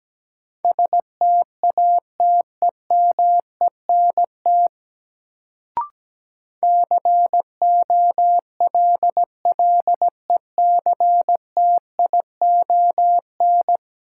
Morse code, heard as a keyed tone, about 17 wpm, 700 Hz. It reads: STATEMENT COLLECTION
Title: Sentences from Top 1200 Words - New Word Drill - 17wpm